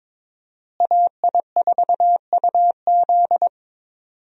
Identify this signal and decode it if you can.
Morse code, heard as a keyed tone, about 22 wpm, 700 Hz. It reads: AI4UZ
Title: USA Call Signs - 5 Characters - Repeated 3x - 22wpm